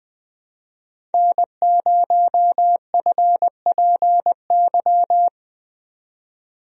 Morse code, heard as a keyed tone, about 20 wpm, 700 Hz. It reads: N0FPY
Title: USA Call Signs - 5 Characters - 20wpm